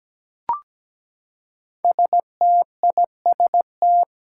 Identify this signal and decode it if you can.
Morse code, heard as a keyed tone, about 17 wpm, 700 Hz. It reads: STIST
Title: Five Challenging Letter-Number Combinations 17wpm